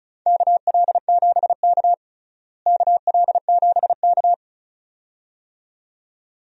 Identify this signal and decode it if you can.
Morse code, heard as a keyed tone, about 36 wpm, 700 Hz. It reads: KL7K KL7K
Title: DXCC-100 Call Signs - 4 Characters - Repeated 2x - 35wpm